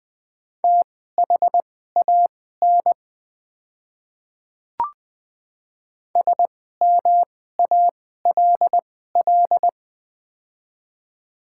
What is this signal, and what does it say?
Morse code, heard as a keyed tone, about 20 wpm, spaced as for 14 wpm, 700 Hz. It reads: THAN SMALL
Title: 2x Character Spacing - Top 500 Words - 20wpm